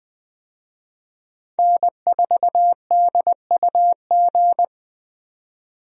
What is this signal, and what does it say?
Morse code, heard as a keyed tone, about 20 wpm, 700 Hz. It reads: N4DUG
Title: USA Call Signs - 5 Characters - 20wpm